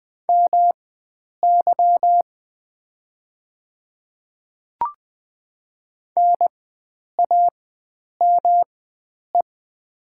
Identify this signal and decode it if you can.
Morse code, heard as a keyed tone, about 20 wpm, spaced as for 9 wpm, 700 Hz. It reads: MY NAME
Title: 4x Character Spacing - Top 300 Words - 20wpm